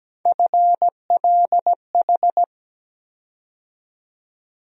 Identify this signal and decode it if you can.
Morse code, heard as a keyed tone, about 17 wpm, 700 Hz. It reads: FLH
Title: USA Call Sign Postfixes - 3 Letters 17wpm